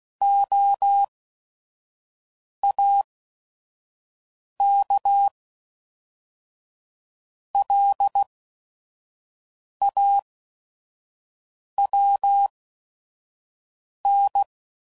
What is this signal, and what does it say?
Morse code, heard as a keyed tone, about 16 wpm, spaced as for 5 wpm, 800 Hz. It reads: OAKLAWN
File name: CW Code Groups @5WPM / fromSM's / Cities